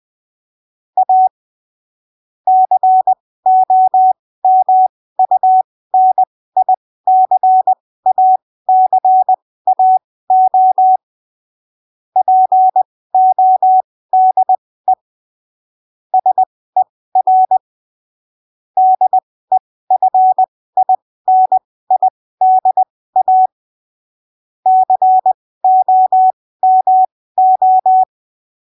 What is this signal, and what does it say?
Morse code, heard as a keyed tone, about 20 wpm, spaced as for 15 wpm, 750 Hz. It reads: A COMUNICACAO PODE SER DEFINIDA COMO
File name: arquivos / texto / texto 18 20ppm